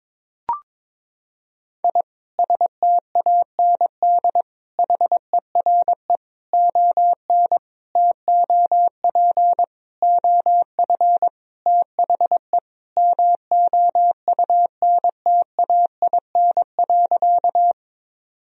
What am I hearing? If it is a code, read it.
Morse code, heard as a keyed tone, about 22 wpm, 700 Hz. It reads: I STAND HERE ON TOP OF THE MOUNTAIN.
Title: Sentences from Top 1000 Words - Review Part 1 - 22wpm